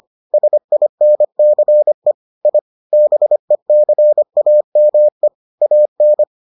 Morse code, heard as a keyed tone, about 25 wpm, 600 Hz. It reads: SINCE I BECAME AN